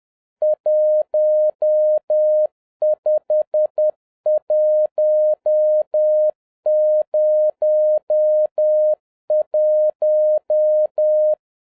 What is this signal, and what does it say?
Morse code, heard as a keyed tone, about 10 wpm, 610 Hz. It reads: 15101